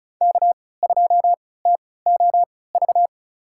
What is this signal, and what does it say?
Morse code, heard as a keyed tone, about 36 wpm, spaced as for 20 wpm, 690 Hz. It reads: K2TOV
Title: USA Call Signs - 5 Characters - 3x Character Spacing - 35wpm